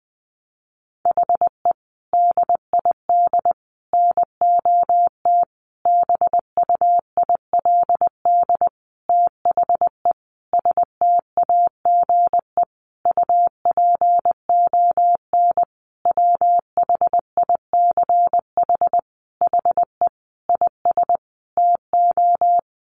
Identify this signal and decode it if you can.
Morse code, heard as a keyed tone, about 20 wpm, 700 Hz. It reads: HE DID NOT BUILD THE STAGE UPON WHICH HE IS TO
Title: Sentences from Top 1200 Words 20wpm